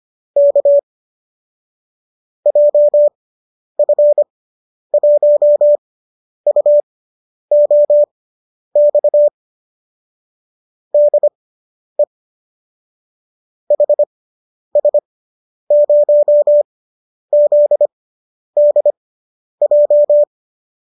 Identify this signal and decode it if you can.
Morse code, heard as a keyed tone, about 25 wpm, spaced as for 10 wpm, 590 Hz. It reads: K JF1UOX DE HS0ZDJ